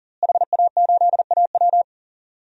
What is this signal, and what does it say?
Morse code, heard as a keyed tone, about 40 wpm, 700 Hz. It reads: HA8AW
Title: DXCC-100 Call Signs - 5 Characters - Repeated 3x - 40wpm